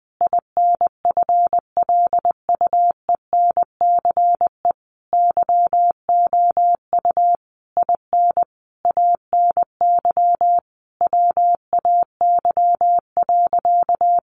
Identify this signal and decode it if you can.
Morse code, heard as a keyed tone, about 19 wpm, 700 Hz. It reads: INFLUENCE YOU IN ANY WAY.
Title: Sentences from Top 1100 Words 20wpm